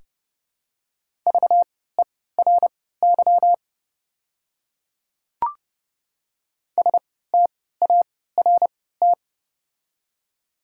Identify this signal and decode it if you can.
Morse code, heard as a keyed tone, about 31 wpm, spaced as for 17 wpm, 700 Hz. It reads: VERY START
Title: 3x Character Spacing - Top 300 Words - 30wpm